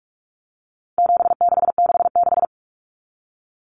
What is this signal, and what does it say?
Morse code, heard as a keyed tone, about 44 wpm, 700 Hz. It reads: 7666
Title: Numbers - 4-Digit 45wpm